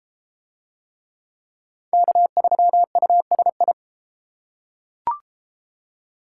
Morse code, heard as a keyed tone, about 31 wpm, 700 Hz. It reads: K3USI